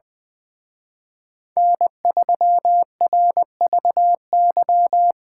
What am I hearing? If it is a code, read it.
Morse code, heard as a keyed tone, about 23 wpm, 700 Hz. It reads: N3RVY
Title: Speed Racing - USA Call Signs 20wpm